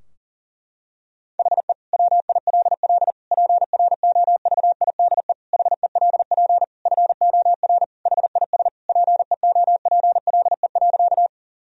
Morse code, heard as a keyed tone, about 40 wpm, 700 Hz. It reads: HE WILL PROVIDE HELP FOR HIS PEOPLE.
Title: Sentences from Top 1000 Words - Review Part 3 - 40wpm